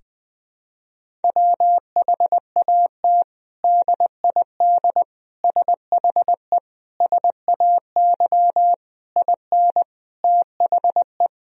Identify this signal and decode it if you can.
Morse code, heard as a keyed tone, about 21 wpm, 700 Hz. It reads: WHAT DID SHE SAY IN THE
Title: Sentences from Top 1000 Words - Review Part 2 - 20wpm